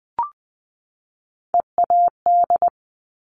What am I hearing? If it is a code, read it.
Morse code, heard as a keyed tone, about 20 wpm, 700 Hz. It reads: EAD
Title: Top Three Letter Combinations in Words 20wpm